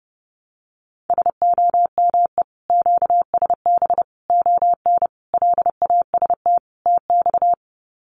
Morse code, heard as a keyed tone, about 30 wpm, 700 Hz. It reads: SOME QSB ON LAST TX
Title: Common QSO Phrases 30wpm